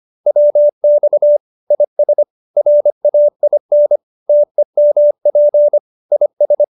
Morse code, heard as a keyed tone, about 25 wpm, 590 Hz. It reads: WX IS RAIN TEMP IS
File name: QSO304 25